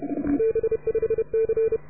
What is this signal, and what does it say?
Morse code, heard as a keyed tone, about 29 wpm, 440 Hz. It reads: B5C